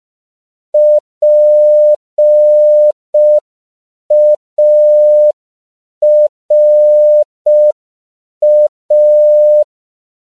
Morse code, heard as a keyed tone, about 5 wpm, 600 Hz. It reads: PARA
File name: CW / 5PPM